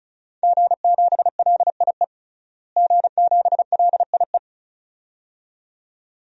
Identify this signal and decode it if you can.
Morse code, heard as a keyed tone, about 36 wpm, 700 Hz. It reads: G7LIE G7LIE
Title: DXCC-100 Call Signs - 5 Characters - Repeated 2x - 35wpm